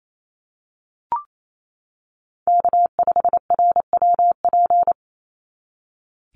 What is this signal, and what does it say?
Morse code, heard as a keyed tone, about 28 wpm, 700 Hz. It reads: K5RWP
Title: NA SOTA Chaser Callsigns 28wpm